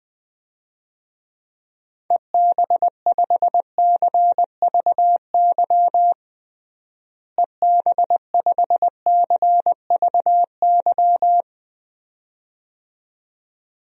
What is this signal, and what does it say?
Morse code, heard as a keyed tone, about 21 wpm, 700 Hz. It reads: EB5CVY EB5CVY
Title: DXCC-100 Call Signs - 6 Characters - Repeated 2x - 20wpm